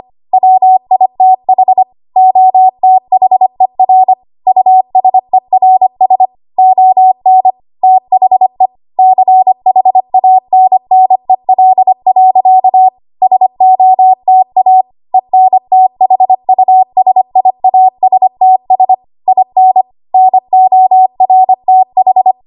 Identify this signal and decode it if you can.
Morse code, heard as a keyed tone, about 25 wpm, 750 Hz. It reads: WITH OTHER USERS ON THE CHANNEL. SOTA ENTHUSIASTS IN NORTH
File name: Morse / 25WPM